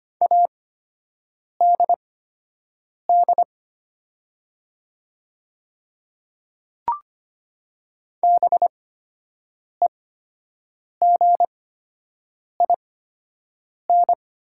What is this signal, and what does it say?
Morse code, heard as a keyed tone, about 25 wpm, spaced as for 7 wpm, 700 Hz. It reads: ADD BEGIN
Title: 8x Character Spacing - Top 300 Words - 25wpm